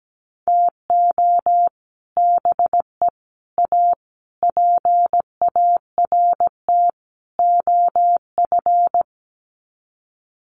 Morse code, heard as a keyed tone, about 17 wpm, 700 Hz. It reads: TO BE A PART OF
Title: Sets of 5 Words 17wpm